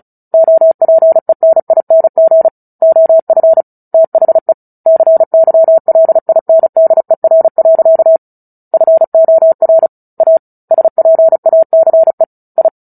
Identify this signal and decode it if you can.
Morse code, heard as a keyed tone, about 34 wpm, 650 Hz. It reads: OPENING OF THE CYLINDER. FOR A SPACE I
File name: Book Two / 09-Chapter 9 - 35 WPM